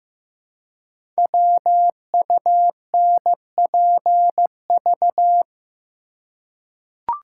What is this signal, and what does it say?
Morse code, heard as a keyed tone, about 15 wpm, 700 Hz. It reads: WUNPV
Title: Code Groups - 5 Letters - 15wpm